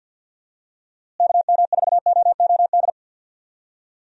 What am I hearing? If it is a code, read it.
Morse code, heard as a keyed tone, about 48 wpm, 700 Hz. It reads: KM4OOD
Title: USA Call Signs - 6 Characters - Repeated 3x - 50wpm